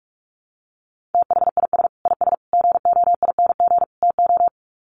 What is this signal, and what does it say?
Morse code, heard as a keyed tone, about 44 wpm, 700 Hz. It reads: THIS IS GOING TO
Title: Sets of 4 Words 45wpm